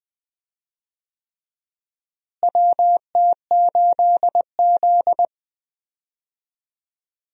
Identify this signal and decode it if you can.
Morse code, heard as a keyed tone, about 20 wpm, 700 Hz. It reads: WT8Z
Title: USA Call Signs - 4 Characters - Repeated 3x - 20wpm